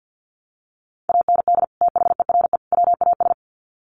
Morse code, heard as a keyed tone, about 50 wpm, 700 Hz. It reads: AND THERE WAS
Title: Sets of 3 Words 50wpm